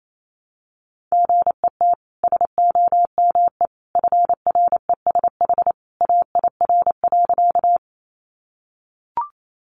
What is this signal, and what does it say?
Morse code, heard as a keyed tone, about 28 wpm, 700 Hz. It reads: GET SOME FRESH AIR.
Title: Sentences from Top 1200 Words 28wpm